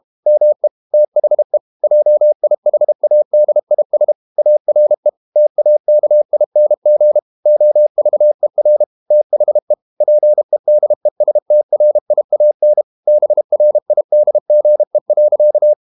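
Morse code, heard as a keyed tone, about 32 wpm, 600 Hz. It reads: ME THE JIHADIS ARE TAKING OVER THE PEDESTRIAN BRIDGE.
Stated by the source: Morse code practice at 32 wpm from CW med Gnister, reading Ken Follett's novel Never in English.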